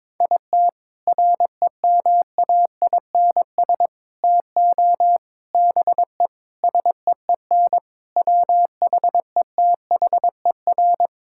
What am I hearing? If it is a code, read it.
Morse code, heard as a keyed tone, about 22 wpm, 700 Hz. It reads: IT REMAINS TO BE SEEN WHETHER
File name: practice / 22wpm / Sets of 6 Words - 22wpm